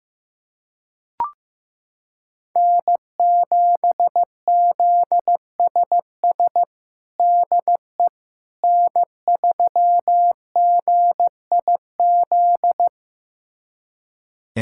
Morse code, heard as a keyed tone, about 15 wpm, 700 Hz. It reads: N7ZSS DE N3GIZ
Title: USA Callsign DE Callsign - 5 Characters - 15wpm